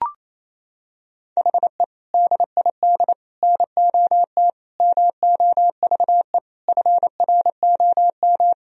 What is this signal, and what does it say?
Morse code, heard as a keyed tone, about 28 wpm, 700 Hz. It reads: HE DID NOT MOVE FROM
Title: Sentences from Top 1000 Words - Review Part 1 - 28wpm